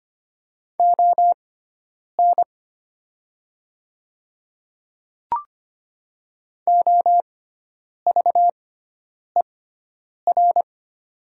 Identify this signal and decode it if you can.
Morse code, heard as a keyed tone, about 25 wpm, spaced as for 9 wpm, 700 Hz. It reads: ON OVER
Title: 6x Character Spacing - Top 100 Words - 25wpm